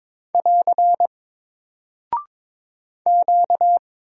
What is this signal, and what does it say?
Morse code, heard as a keyed tone, about 22 wpm, 700 Hz. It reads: <AR> Q
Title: Single Character (Letter:Number:Punctuation:Prosign) Alt 22wpm